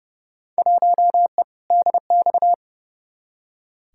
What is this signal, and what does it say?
Morse code, heard as a keyed tone, about 30 wpm, 700 Hz. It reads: 1E DX
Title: Field Day Exchanges Encore - 30wpm